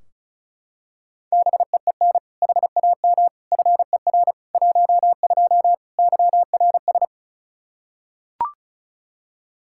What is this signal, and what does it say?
Morse code, heard as a keyed tone, about 36 wpm, 700 Hz. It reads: BEEN HAM FER 12 YRS
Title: Practice QSO - Exchange 3 - More Info - 35wpm